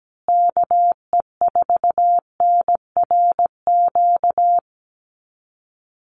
Morse code, heard as a keyed tone, about 17 wpm, 700 Hz. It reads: KE4NRQ
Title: USA Call Signs Repeated 2x 17wpm